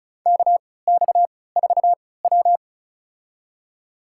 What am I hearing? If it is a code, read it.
Morse code, heard as a keyed tone, about 36 wpm, spaced as for 20 wpm, 700 Hz. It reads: KX4W
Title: USA Call Signs - 3x Character Spacing - 35wpm